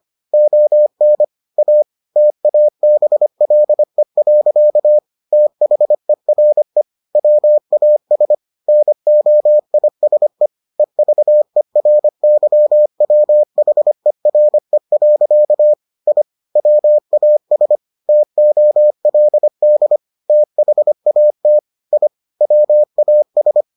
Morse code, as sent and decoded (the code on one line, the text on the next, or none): --- -. / .- / - .- -... .-.. .-.-.- / - .... .-. / .-- .- ... / -. --- .. ... / . ...- . .-. -.-- .-- .... .-. .-.-.- / .. / .-- .- ... / - --- .-.. -.. / - .... .- - / .. / .-- .- ...
ON A TABLE. THERE WAS NOISE EVERYWHERE. I WAS TOLD THAT I WAS